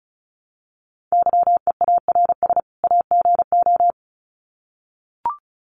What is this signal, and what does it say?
Morse code, heard as a keyed tone, about 34 wpm, 700 Hz. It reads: YEARS AGO
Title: Sets of 2 Words 35wpm